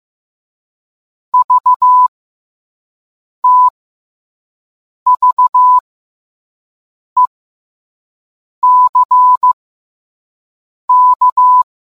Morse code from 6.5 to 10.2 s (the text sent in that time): E C